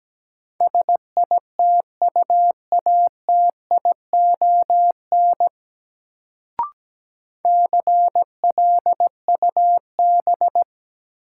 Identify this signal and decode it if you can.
Morse code, heard as a keyed tone, about 17 wpm, 700 Hz. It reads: SITUATION CLUB
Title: Sentences from Top 1200 Words - New Word Drill Alt - 17wpm